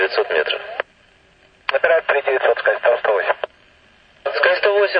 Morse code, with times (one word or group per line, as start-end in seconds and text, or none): none